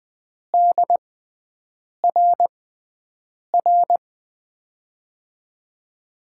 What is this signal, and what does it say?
Morse code, heard as a keyed tone, about 20 wpm, 700 Hz. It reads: D R R
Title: USA Call Signs - 6x Character Spacing - 20wpm